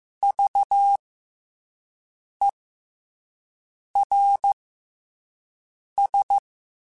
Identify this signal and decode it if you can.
Morse code, heard as a keyed tone, about 15 wpm, 800 Hz. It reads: V E R S
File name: Special INGs-mp3s / Special INGs-mp3s / SlowToFaster ing @10